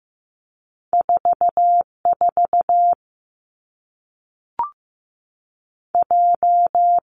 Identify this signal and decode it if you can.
Morse code, heard as a keyed tone, about 15 wpm, 700 Hz. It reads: 4 4 J
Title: Two Challenging Letter-Number Combinations 15wpm